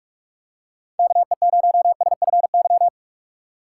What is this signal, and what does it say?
Morse code, heard as a keyed tone, about 46 wpm, 700 Hz. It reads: KE0SFY
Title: USA Call Signs - 6 Characters - 2x Character Spacing - 45wpm